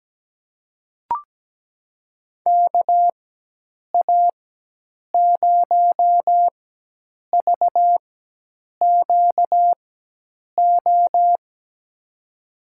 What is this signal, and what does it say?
Morse code, heard as a keyed tone, about 17 wpm, spaced as for 8 wpm, 700 Hz. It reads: KA0VQO
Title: USA Call Signs - 6 Characters - 4x Character Spacing - 17wpm